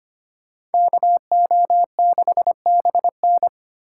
Morse code, heard as a keyed tone, about 31 wpm, 700 Hz. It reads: KO6BN
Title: Speed Racing - USA 2x2 Call Signs 25wpm